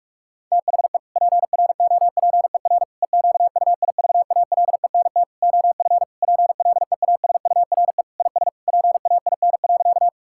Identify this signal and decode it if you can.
Morse code, heard as a keyed tone, about 42 wpm, 700 Hz. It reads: THE PROPER EQUIVALENT OF PLEASURE IS PAIN.
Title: Sentences from Top 1600 Words 45wpm